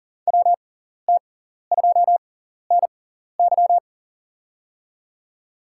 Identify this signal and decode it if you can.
Morse code, heard as a keyed tone, about 40 wpm, spaced as for 14 wpm, 700 Hz. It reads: WT2NY
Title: USA Call Signs - 6x Character Spacing - 40wpm